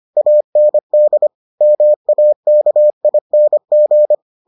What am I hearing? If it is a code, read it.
Morse code, heard as a keyed tone, about 25 wpm, 600 Hz. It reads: AND MAKING